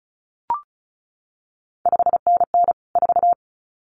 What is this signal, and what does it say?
Morse code, heard as a keyed tone, about 36 wpm, 700 Hz. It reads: 5NN 4
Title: CQ WW DX Contest Exchange - 35wpm